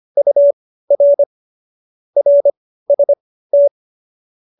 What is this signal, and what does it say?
Morse code, heard as a keyed tone, about 25 wpm, spaced as for 15 wpm, 570 Hz. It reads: UR RST
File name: qso222 15